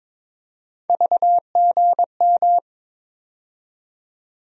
Morse code, heard as a keyed tone, about 22 wpm, 700 Hz. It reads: VGM
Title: USA Call Sign Postfixes - 3 Letters 22wpm